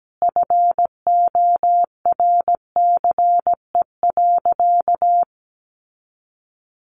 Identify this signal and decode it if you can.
Morse code, heard as a keyed tone, about 17 wpm, 700 Hz. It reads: FORCE.
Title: Sentences from Top 1200 Words 17wpm